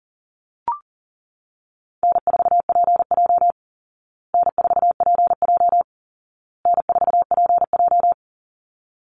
Morse code, heard as a keyed tone, about 40 wpm, 700 Hz. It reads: N4PJ N4PJ N4PJ
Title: Easy Call Signs Repeated 3x - 40wpm